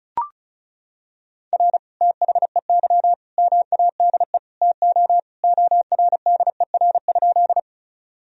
Morse code, heard as a keyed tone, about 36 wpm, 700 Hz. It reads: R THEY MADE TO ORDER?
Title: Sentences from Top 400 Words and Abbreviations 35wpm